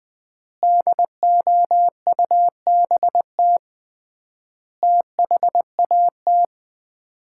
Morse code, heard as a keyed tone, about 20 wpm, 700 Hz. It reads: DOUBT THAT
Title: 3x Word-Spacing - Sets of 5 Words - 20wpm